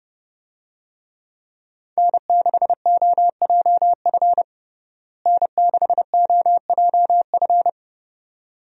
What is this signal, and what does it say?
Morse code, heard as a keyed tone, about 32 wpm, 700 Hz. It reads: N6OJF N6OJF
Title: USA Call Signs - 5 Characters - Repeated 2x - 30wpm